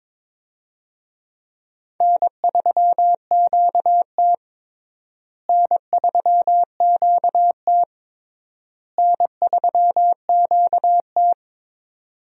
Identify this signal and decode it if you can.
Morse code, heard as a keyed tone, about 22 wpm, 700 Hz. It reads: N3QT N3QT N3QT
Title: USA Call Signs - 4 Characters - Repeated 3x - 22wpm